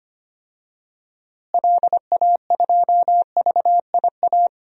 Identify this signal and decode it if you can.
Morse code, heard as a keyed tone, about 25 wpm, 700 Hz. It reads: LA2VIA
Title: DXCC-100 Call Signs - 6 Characters - Repeated 3x - 25wpm